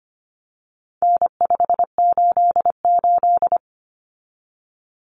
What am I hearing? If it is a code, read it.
Morse code, heard as a keyed tone, about 25 wpm, 700 Hz. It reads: N588
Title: Common Cut Numbers - 4-Digit 25wpm